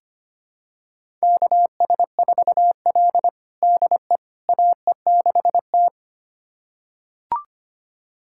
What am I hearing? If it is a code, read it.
Morse code, heard as a keyed tone, about 25 wpm, 700 Hz. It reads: KS4L DE AE6T
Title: USA Callsign DE Callsign - 4 Characters - 25wpm